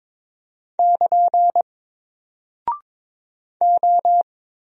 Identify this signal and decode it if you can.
Morse code, heard as a keyed tone, about 22 wpm, 700 Hz. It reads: <KN> O
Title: Single Character (Letter:Number:Punctuation:Prosign) 22wpm